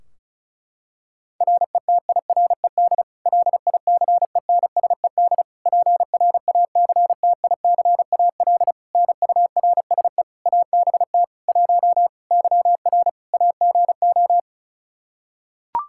Morse code, heard as a keyed tone, about 36 wpm, 700 Hz. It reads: RETIRED LICENSED PRACTICAL NURSE ABT 1 YR AGO
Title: Practice QSO - Exchange 3 - More Info - 35wpm